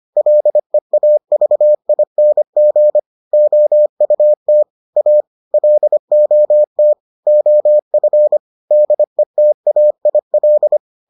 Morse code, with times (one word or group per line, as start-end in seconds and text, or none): 0.2-3.0 LEAVING
3.3-4.6 OUT
5.0-5.2 A
5.5-6.9 LOT
7.3-8.4 OF
8.7-10.8 DETAIL